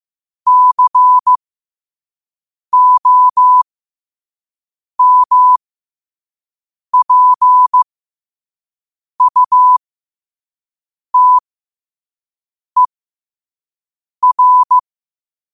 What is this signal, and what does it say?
Morse code, heard as a keyed tone, about 15 wpm, spaced as for 5 wpm, 1000 Hz. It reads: COMPUTER
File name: qotdfiles / 5wpm